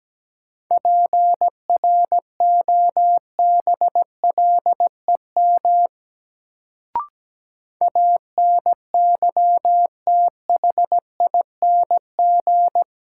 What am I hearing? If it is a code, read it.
Morse code, heard as a keyed tone, about 17 wpm, 700 Hz. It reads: PROBLEM ANYTHING
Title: Sentences from Top 1000 Words - New Word Drill - 17wpm